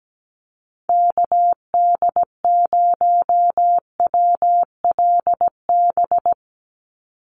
Morse code, heard as a keyed tone, about 17 wpm, 700 Hz. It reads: KD0WLB
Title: USA Call Signs 17wpm